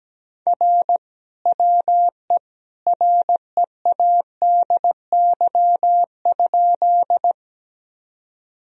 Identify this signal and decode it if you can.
Morse code, heard as a keyed tone, about 17 wpm, 700 Hz. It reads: R WE READY?
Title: Sentences from Top 400 Words and Abbreviations 17wpm